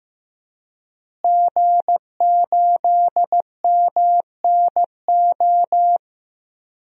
Morse code, heard as a keyed tone, about 15 wpm, 700 Hz. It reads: G8MNO
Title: DXCC-100 Call Signs - 5 Characters - Repeated 2x - 15wpm